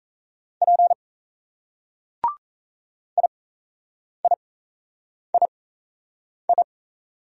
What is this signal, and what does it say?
Morse code, heard as a keyed tone, about 38 wpm, spaced as for 8 wpm, 700 Hz. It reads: P IIII